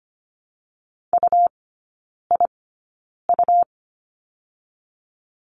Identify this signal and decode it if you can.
Morse code, heard as a keyed tone, about 25 wpm, spaced as for 9 wpm, 700 Hz. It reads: UIU
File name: Lesson 17e - Letter u - Three Characters 25wpm